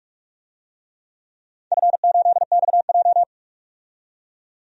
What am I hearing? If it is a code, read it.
Morse code, heard as a keyed tone, about 41 wpm, 700 Hz. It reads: F8XJ
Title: DXCC-100 Call Signs - 4 Characters - 45wpm